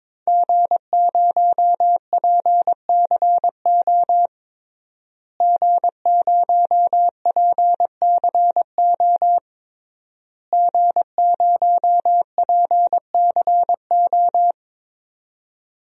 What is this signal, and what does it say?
Morse code, heard as a keyed tone, about 22 wpm, 700 Hz. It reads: G0PCO G0PCO G0PCO
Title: DXCC-100 Call Signs - 5 Characters - Repeated 3x - 22wpm